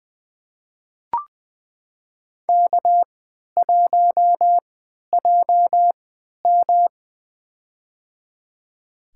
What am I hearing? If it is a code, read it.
Morse code, heard as a keyed tone, about 21 wpm, spaced as for 11 wpm, 700 Hz. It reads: K1JM
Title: USA Call Signs - 4 Characters - 3x Character Spacing - 20wpm